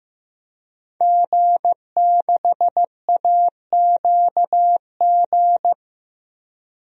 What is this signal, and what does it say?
Morse code, heard as a keyed tone, about 16 wpm, 700 Hz. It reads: G6AQG
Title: DXCC-100 Call Signs - 5 Characters - Repeated 3x - 15wpm